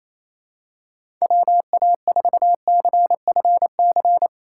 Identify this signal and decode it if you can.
Morse code, heard as a keyed tone, about 28 wpm, 700 Hz. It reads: WA4CFC
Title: USA Call Signs - 6 Characters - 4x Character Spacing - 28wpm